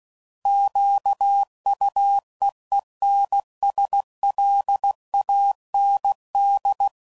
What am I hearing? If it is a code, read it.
Morse code, heard as a keyed tone, about 16 wpm, 800 Hz. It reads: QUEENSLAND
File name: CW Code Groups @15WPM / fromSM's / Cities